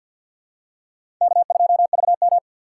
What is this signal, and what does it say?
Morse code, heard as a keyed tone, about 47 wpm, 700 Hz. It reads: K2VM
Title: USA Call Signs - 2x Character Spacing - 50wpm